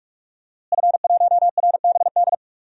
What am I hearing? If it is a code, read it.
Morse code, heard as a keyed tone, about 44 wpm, 700 Hz. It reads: F1RBD
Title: DXCC-100 Call Signs - 5 Characters - Repeated 2x - 45wpm